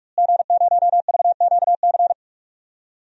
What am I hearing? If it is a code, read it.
Morse code, heard as a keyed tone, about 46 wpm, 700 Hz. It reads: G0VQC
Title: DXCC-100 Call Signs - 5 Characters - Repeated 2x - 45wpm